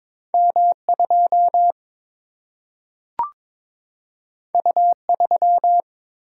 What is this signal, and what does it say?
Morse code, heard as a keyed tone, about 22 wpm, 700 Hz. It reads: M 2 U 3
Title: Single-Letter Single-Number 22wpm